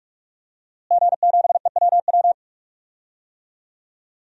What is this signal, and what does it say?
Morse code, heard as a keyed tone, about 46 wpm, 700 Hz. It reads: G7EWW
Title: DXCC-100 Call Signs - 5 Characters - Repeated 2x - 45wpm